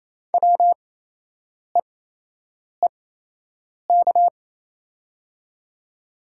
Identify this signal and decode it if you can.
Morse code, heard as a keyed tone, about 28 wpm, spaced as for 8 wpm, 700 Hz. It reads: WEEK
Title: 8x Character Spacing - Top 500 Words - 28wpm